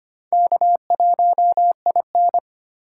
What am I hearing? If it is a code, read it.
Morse code, heard as a keyed tone, about 29 wpm, 700 Hz. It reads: K1IN
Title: Speed Racing - USA 1x2 Call Signs 25wpm